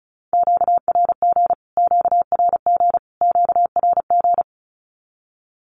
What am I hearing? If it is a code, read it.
Morse code, heard as a keyed tone, about 34 wpm, 700 Hz. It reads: QRG QRG QRG
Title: Q-Codes Repeated 3x 35wpm